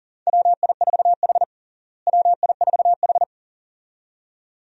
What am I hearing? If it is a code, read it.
Morse code, heard as a keyed tone, about 37 wpm, 700 Hz. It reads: WI4H WI4H
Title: USA Call Signs - 4 Characters - Repeated 2x - 40wpm